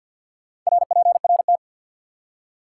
Morse code, heard as a keyed tone, about 53 wpm, 700 Hz. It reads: RPRT
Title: Top 90 QSO Elements 50wpm